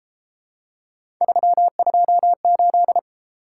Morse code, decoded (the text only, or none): 328